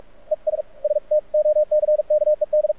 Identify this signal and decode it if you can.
Morse code, heard as a keyed tone, about 46 wpm, spaced as for 40 wpm, 610 Hz. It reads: ES STOCKED
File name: archived / CRSnet